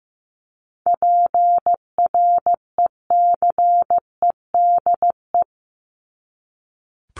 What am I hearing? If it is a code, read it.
Morse code, heard as a keyed tone, about 15 wpm, 700 Hz. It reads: PRECEDE
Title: ING Suffix - Top 100 Words with Base Words 15wpm